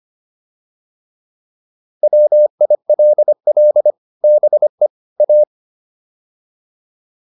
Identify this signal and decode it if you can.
Morse code, heard as a keyed tone, about 25 wpm, 600 Hz. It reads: WILL BE A